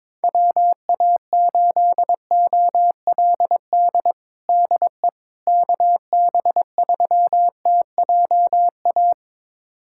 Morse code, heard as a keyed tone, about 22 wpm, 700 Hz. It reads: WA8OLD DE KB3TJA
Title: USA Callsign DE Callsign - 6 Characters - 22wpm